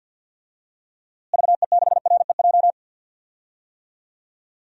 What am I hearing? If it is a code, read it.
Morse code, heard as a keyed tone, about 46 wpm, 700 Hz. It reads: VE6REJ
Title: DXCC-100 Call Signs - 6 Characters - Repeated 3x - 50wpm